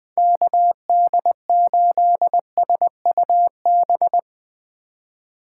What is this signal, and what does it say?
Morse code, heard as a keyed tone, about 20 wpm, 700 Hz. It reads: KD8SUB